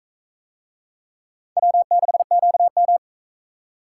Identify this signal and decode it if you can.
Morse code, heard as a keyed tone, about 40 wpm, 700 Hz. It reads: W6QM